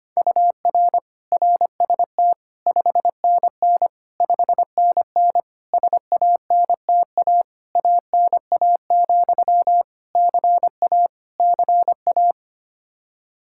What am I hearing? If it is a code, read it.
Morse code, heard as a keyed tone, about 25 wpm, 700 Hz. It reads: UR RST 5NN 5NN SANTA ANA, CA CA
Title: Practice QSO - Exchange 1 - Beginning - 25wpm